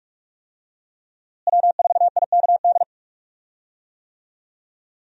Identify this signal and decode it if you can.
Morse code, heard as a keyed tone, about 46 wpm, 700 Hz. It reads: W4IKD
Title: USA Call Signs - 5 Characters - Repeated 2x - 45wpm